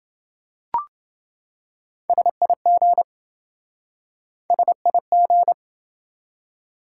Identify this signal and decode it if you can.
Morse code, heard as a keyed tone, about 28 wpm, 700 Hz. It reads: SIG SIG